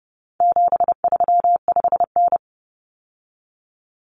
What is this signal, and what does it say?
Morse code, heard as a keyed tone, about 30 wpm, 700 Hz. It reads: 735N
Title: Common Cut Numbers - 4-Digit 30wpm